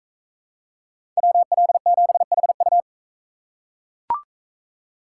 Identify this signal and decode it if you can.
Morse code, heard as a keyed tone, about 40 wpm, 700 Hz. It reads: WL7HU